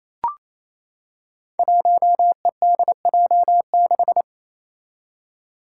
Morse code, heard as a keyed tone, about 28 wpm, 700 Hz. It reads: 1EDJ6
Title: Five Challenging Letter-Number Combinations 28wpm